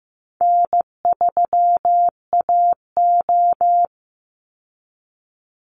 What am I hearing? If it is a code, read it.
Morse code, heard as a keyed tone, about 15 wpm, 700 Hz. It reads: N3AO
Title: Easy Call Signs Repeated 2x - 15wpm